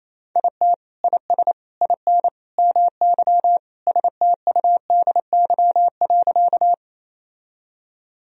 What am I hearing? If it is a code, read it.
Morse code, heard as a keyed tone, about 28 wpm, 700 Hz. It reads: IT IS IN MY STUDY.
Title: Sentences from Top 1000 Words - Review Part 1 - 28wpm